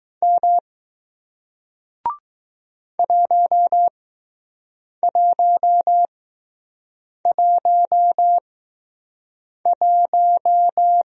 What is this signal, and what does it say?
Morse code, heard as a keyed tone, about 21 wpm, spaced as for 6 wpm, 700 Hz. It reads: M 1111